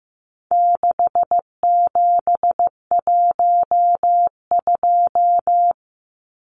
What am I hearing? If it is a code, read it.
Morse code, heard as a keyed tone, about 15 wpm, 700 Hz. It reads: 6712
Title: Numbers - 4-Digit 15wpm